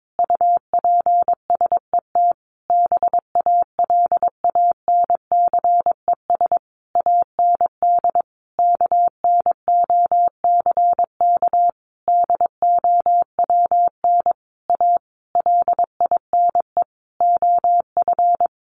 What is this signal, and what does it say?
Morse code, heard as a keyed tone, about 22 wpm, 700 Hz. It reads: UPSET BALANCES AND KNOCK DOWN A LINE OF
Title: A Sound Of Thunder 22wpm